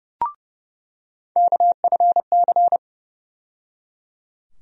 Morse code, heard as a keyed tone, about 30 wpm, 700 Hz. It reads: KFC
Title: USA Call Sign Postfixes - 3 Letters 30wpm